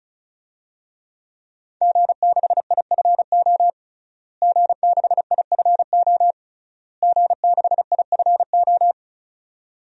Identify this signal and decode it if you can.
Morse code, heard as a keyed tone, about 34 wpm, 700 Hz. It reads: G6IFO G6IFO G6IFO
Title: DXCC-100 Call Signs - 5 Characters - Repeated 3x - 35wpm